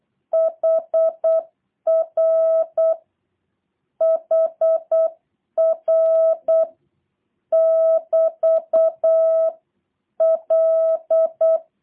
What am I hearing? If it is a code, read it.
Morse code, heard as a keyed tone, about 8 wpm, 650 Hz. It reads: HR HR <BT> L
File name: jh8cbh / 2023cw / review1-6